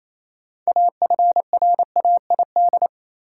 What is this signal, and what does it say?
Morse code, heard as a keyed tone, about 27 wpm, 700 Hz. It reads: AFRAID
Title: Sentences from Top 1500 Words - New Word Drill Alt - 28wpm